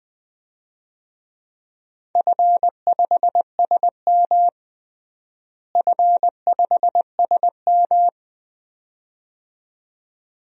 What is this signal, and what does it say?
Morse code, heard as a keyed tone, about 20 wpm, 700 Hz. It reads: F5SM F5SM
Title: DXCC-100 Call Signs - 4 Characters - Repeated 2x - 20wpm